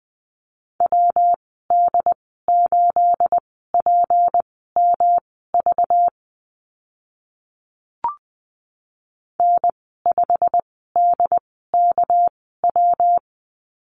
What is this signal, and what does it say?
Morse code, heard as a keyed tone, about 20 wpm, spaced as for 14 wpm, 700 Hz. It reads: WD8PMV N5DKW
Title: 219 Lesson 35g - Number 0 - Call Signs 20wpm